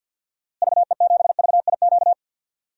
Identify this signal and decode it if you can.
Morse code, heard as a keyed tone, about 52 wpm, 700 Hz. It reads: VE7VIQ
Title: DXCC-100 Call Signs - 6 Characters - Repeated 2x - 50wpm